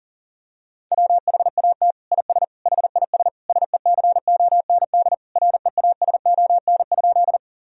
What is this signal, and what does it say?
Morse code, heard as a keyed tone, about 40 wpm, 700 Hz. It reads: WHAT IS HIS SECOND REASON?
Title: Sentences from Top 1000 Words - Review Part 1 - 40wpm